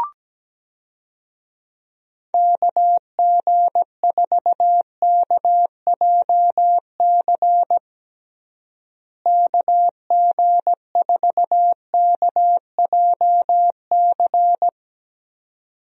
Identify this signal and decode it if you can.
Morse code, heard as a keyed tone, about 17 wpm, 700 Hz. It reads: KG4KJC KG4KJC
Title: USA Call Signs - 6 Characters - Repeated 2x - 17wpm